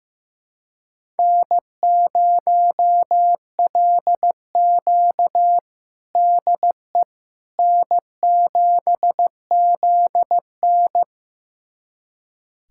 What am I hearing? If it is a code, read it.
Morse code, heard as a keyed tone, about 16 wpm, 700 Hz. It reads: N0LQ DE N7ZN
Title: USA Callsign DE Callsign - 4 Characters - 15wpm